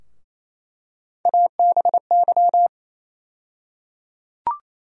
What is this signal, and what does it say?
Morse code, heard as a keyed tone, about 28 wpm, 700 Hz. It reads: ABY
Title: USA Call Sign Postfixes - 3 Letters 28wpm